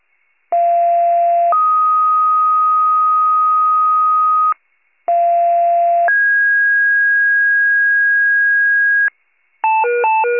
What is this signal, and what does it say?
Two-tone page, heard as a one-second tone, then a three-second tone, about 690 then 1250 Hz, about 690 then 1650 Hz, with noise. It is heard from 0.5 to 9.1 s.
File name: wcalert